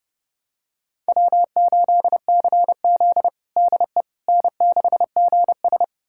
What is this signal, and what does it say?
Morse code, heard as a keyed tone, about 32 wpm, 700 Hz. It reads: W8CZ DE N6GS
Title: USA Callsign DE Callsign - 4 Characters - 30wpm